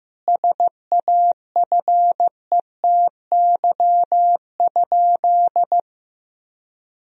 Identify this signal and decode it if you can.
Morse code, heard as a keyed tone, about 15 wpm, 700 Hz. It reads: SAFETY?
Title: Sentences from Top 1000 Words - Review Part 4 - 15wpm